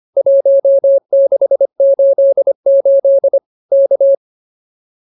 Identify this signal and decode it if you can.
Morse code, heard as a keyed tone, about 24 wpm, 550 Hz. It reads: 1688 K